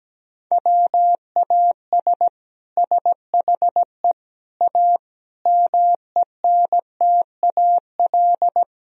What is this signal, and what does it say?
Morse code, heard as a keyed tone, about 17 wpm, 700 Hz. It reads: WAS SHE A MENTAL
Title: Sentences from Top 1000 Words - Review Part 4 - 17wpm